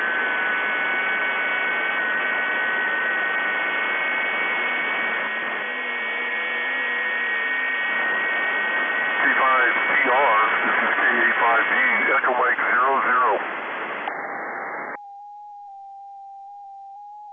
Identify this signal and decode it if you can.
FT8, with speech.